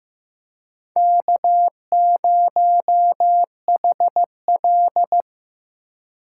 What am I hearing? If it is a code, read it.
Morse code, heard as a keyed tone, about 15 wpm, 700 Hz. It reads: K0HL
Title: USA Call Signs - 4 Characters - 2x Character Spacing - 15wpm